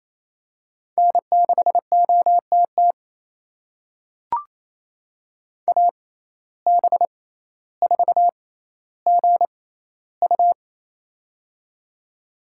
Morse code, heard as a keyed tone, about 28 wpm, spaced as for 14 wpm, 700 Hz. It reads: N6OTT AB4GU
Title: USA Call Signs - 5 Characters - 6x Character Spacing - 28wpm